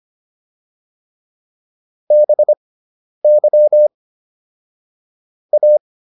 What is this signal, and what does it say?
Morse code, heard as a keyed tone, about 25 wpm, spaced as for 10 wpm, 600 Hz. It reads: BY A